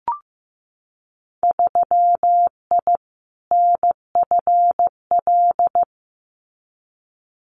Morse code, heard as a keyed tone, about 15 wpm, 700 Hz. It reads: 3I NFL
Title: Winter Field Day Exchanges 15wpm